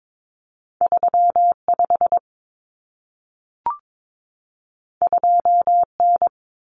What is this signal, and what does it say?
Morse code, heard as a keyed tone, about 22 wpm, 700 Hz. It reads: 35 2N
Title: Common Cut Numbers - 2-Digit 22wpm